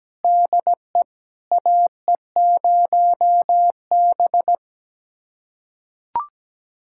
Morse code, heard as a keyed tone, about 17 wpm, 700 Hz. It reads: DE AE0B
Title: USA Callsign DE Callsign - 4 Characters - 17wpm